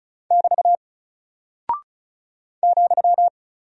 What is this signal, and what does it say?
Morse code, heard as a keyed tone, about 36 wpm, 700 Hz. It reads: <BT> ,
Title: Single Character (Letter:Number:Punctuation:Prosign) 35wpm